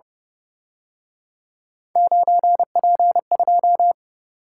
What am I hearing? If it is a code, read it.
Morse code, heard as a keyed tone, about 30 wpm, 700 Hz. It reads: 9P2
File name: Speed-Racing - Three Letter-Number Combinations - 25wpm